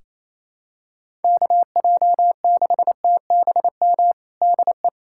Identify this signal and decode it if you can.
Morse code, heard as a keyed tone, about 28 wpm, 700 Hz. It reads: KJ6TBM DE
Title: USA Callsign DE Callsign 28wpm